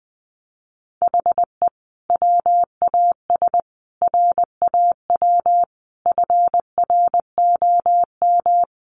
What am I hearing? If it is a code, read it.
Morse code, heard as a keyed tone, about 20 wpm, 700 Hz. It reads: HE WAS RAW FROM
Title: Sentences from Top 1500 Words 20wpm